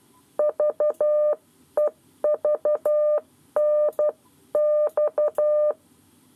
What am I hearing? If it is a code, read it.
Morse code, heard as a keyed tone, about 12 wpm, 600 Hz. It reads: VEVNX
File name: 7 Lezione cw 1